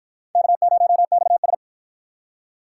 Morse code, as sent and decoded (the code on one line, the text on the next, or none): -.- ----- -..- ...
K0XS